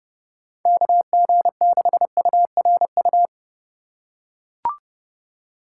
Morse code, heard as a keyed tone, about 30 wpm, 700 Hz. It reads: KG6URU
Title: USA Call Signs - 6 Characters - 2x Character Spacing - 30wpm